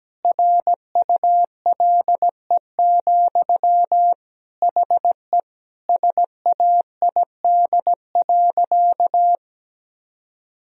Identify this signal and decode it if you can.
Morse code, heard as a keyed tone, about 17 wpm, 700 Hz. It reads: RULE, HE SAID.
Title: Sentences from Top 1000 Words - Review Part 2 - 17wpm